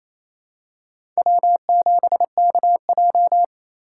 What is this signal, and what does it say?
Morse code, heard as a keyed tone, about 28 wpm, 700 Hz. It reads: W7KJ
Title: USA Call Signs - 4 Characters - 4x Character Spacing - 28wpm